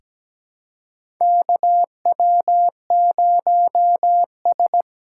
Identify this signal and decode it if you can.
Morse code, heard as a keyed tone, about 17 wpm, 700 Hz. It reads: KW0S
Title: USA Call Signs - 4x Character Spacing - 17wpm